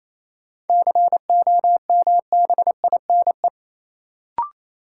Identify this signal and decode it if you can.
Morse code, heard as a keyed tone, about 28 wpm, 700 Hz. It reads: COMBINE
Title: Sentences from Top 1600 Words - New Word Drill - 28wpm